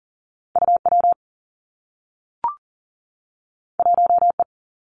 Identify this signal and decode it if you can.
Morse code, heard as a keyed tone, about 40 wpm, 700 Hz. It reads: U W 1 E
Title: Two Challenging Letter-Number Combinations 40wpm